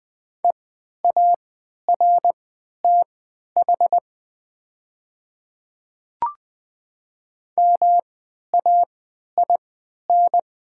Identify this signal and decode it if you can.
Morse code, heard as a keyed tone, about 20 wpm, spaced as for 11 wpm, 700 Hz. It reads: EARTH MAIN
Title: 3x Character Spacing - Top 300 Words - 20wpm